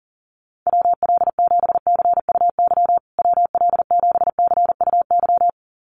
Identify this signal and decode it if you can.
Morse code, heard as a keyed tone, about 40 wpm, 700 Hz. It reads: WL7CUY WL7CUY
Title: USA Call Signs Repeated 2x 40wpm